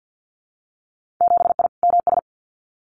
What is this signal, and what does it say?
Morse code, heard as a keyed tone, about 50 wpm, 700 Hz. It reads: 7I MS
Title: Winter Field Day Exchanges Encore 50wpm